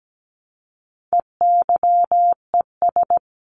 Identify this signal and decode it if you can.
Morse code, heard as a keyed tone, about 17 wpm, 700 Hz. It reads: EYES
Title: A Sound Of Thunder 17wpm